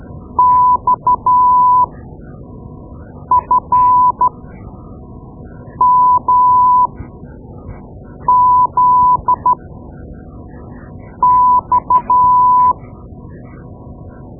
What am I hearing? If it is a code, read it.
Morse code, heard as a keyed tone, about 12 wpm, spaced as for 5 wpm, 1000 Hz. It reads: XFMZX
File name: Morse Lessons / Morse Lessons / Tape-2B